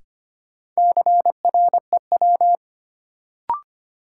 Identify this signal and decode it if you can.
Morse code, heard as a keyed tone, about 25 wpm, 700 Hz. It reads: CREW
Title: Sentences from Top 1500 Words - New Word Drill - 25wpm